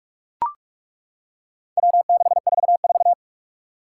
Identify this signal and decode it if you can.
Morse code, heard as a keyed tone, about 41 wpm, 700 Hz. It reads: W644
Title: Four Challenging Letter-Number Combinations 45wpm